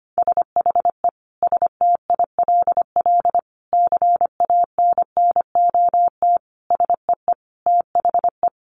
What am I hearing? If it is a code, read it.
Morse code, heard as a keyed tone, about 25 wpm, 700 Hz. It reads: SHE STILL CANNOT SEE THE
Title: Additional Easy Phrases 25wpm